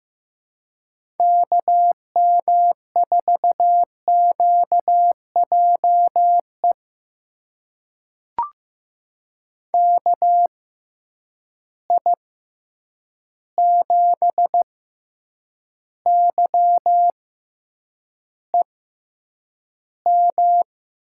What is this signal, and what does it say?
Morse code, heard as a keyed tone, about 16 wpm, spaced as for 8 wpm, 700 Hz. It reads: KM4QJE KI7YEM
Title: USA Call Signs - 6 Characters - 6x Character Spacing - 15wpm